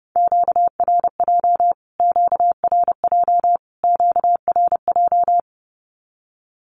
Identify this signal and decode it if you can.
Morse code, heard as a keyed tone, about 28 wpm, 700 Hz. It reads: QRJ QRJ QRJ
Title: Q-Codes Repeated 3x 30wpm